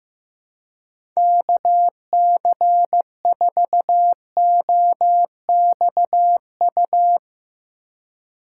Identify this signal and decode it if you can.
Morse code, heard as a keyed tone, about 15 wpm, 700 Hz. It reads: KC4OXU
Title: USA Call Signs - 6 Characters - 2x Character Spacing - 15wpm